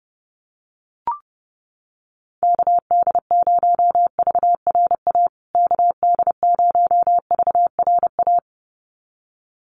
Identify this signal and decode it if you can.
Morse code, heard as a keyed tone, about 30 wpm, 700 Hz. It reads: KD0VRA KD0VRA
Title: USA Call Signs Repeated 2x 30wpm